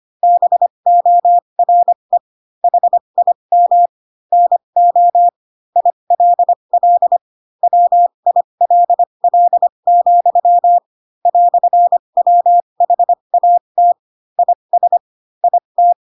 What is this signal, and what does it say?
Morse code, heard as a keyed tone, about 25 wpm, spaced as for 22 wpm, 700 Hz. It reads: BORE HIM NO ILL WILL, "WHAT IS IT
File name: Pinocchio / Pinocchio ch2 18 25 ik1hge